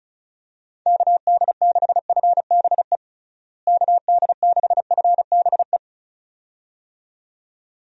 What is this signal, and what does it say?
Morse code, heard as a keyed tone, about 36 wpm, 700 Hz. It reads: KD6FBE KD6FBE
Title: USA Call Signs - 6 Characters - Repeated 2x - 35wpm